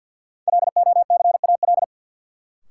Morse code, heard as a keyed tone, about 53 wpm, 700 Hz. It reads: LOCAL
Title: Sentences from Top 1500 Words - New Word Drill Alt - 50wpm